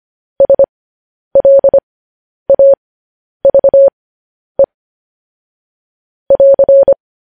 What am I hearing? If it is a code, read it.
Morse code, heard as a keyed tone, about 25 wpm, spaced as for 10 wpm, 550 Hz. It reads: SLAVE <AR>
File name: podcast / News headlines-1x-25x10